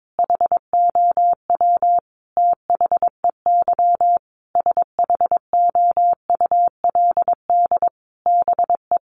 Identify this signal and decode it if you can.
Morse code, heard as a keyed tone, about 22 wpm, 700 Hz. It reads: HOW THEY SHOULD BE
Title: Sentences from Top 1300 Words 22wpm